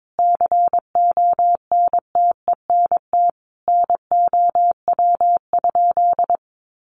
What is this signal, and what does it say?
Morse code, heard as a keyed tone, about 22 wpm, 700 Hz. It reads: CONTENT NOW?
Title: Sentences from Top 900 Words 22wpm